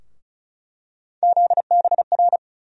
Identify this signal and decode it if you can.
Morse code, heard as a keyed tone, about 34 wpm, 700 Hz. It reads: ZBR
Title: USA Call Sign Postfixes - 3 Letters 35wpm